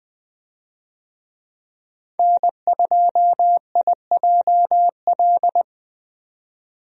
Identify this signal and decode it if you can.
Morse code, heard as a keyed tone, about 20 wpm, 700 Hz. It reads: N2IJL
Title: USA Call Signs - 5 Characters - 20wpm